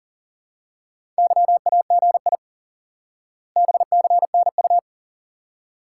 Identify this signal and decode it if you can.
Morse code, heard as a keyed tone, about 37 wpm, 700 Hz. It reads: YAGI BCNU
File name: IQE Course 3 - Lesson 89 - YAGI - Two Elements - 40wpm